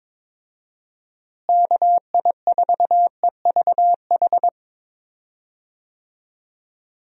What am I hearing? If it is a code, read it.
Morse code, heard as a keyed tone, about 21 wpm, 700 Hz. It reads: KI4EVH
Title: USA Call Signs - 6 Characters - Repeated 3x - 22wpm